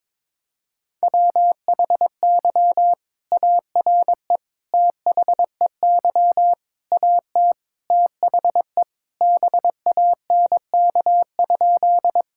Morse code, heard as a keyed tone, about 22 wpm, 700 Hz. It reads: WHY ARE THEY AT THE BANK?
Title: Sentences from Top 1000 Words - Review Part 6 - 22wpm